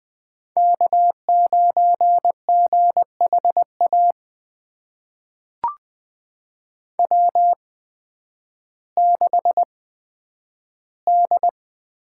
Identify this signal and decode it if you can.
Morse code, heard as a keyed tone, about 20 wpm, spaced as for 11 wpm, 700 Hz. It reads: K9GHA W6D
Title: USA Call Signs - 5 Characters - 8x Character Spacing - 20wpm